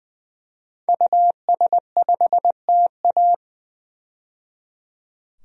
Morse code, heard as a keyed tone, about 20 wpm, 700 Hz. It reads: US5TA
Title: DXCC-100 Call Signs - 5 Characters - 20wpm